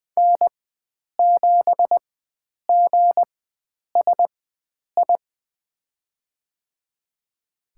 Morse code, heard as a keyed tone, about 20 wpm, spaced as for 9 wpm, 700 Hz. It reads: N7GSI
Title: USA Call Signs - 5 Characters - 4x Character Spacing - 20wpm